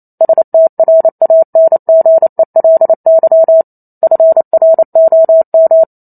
Morse code, heard as a keyed tone, about 28 wpm, 650 Hz. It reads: STRANGELY FROM